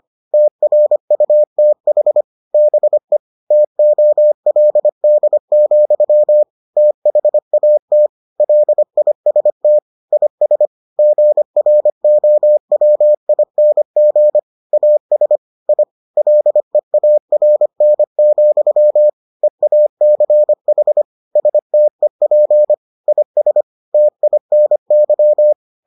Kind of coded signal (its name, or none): Morse code